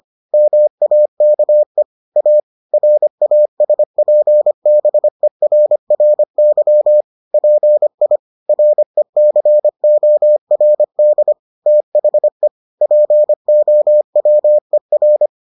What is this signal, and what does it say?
Morse code, heard as a keyed tone, about 25 wpm, 600 Hz. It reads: MAKE A RASPBERRY PI RECORD THE POWER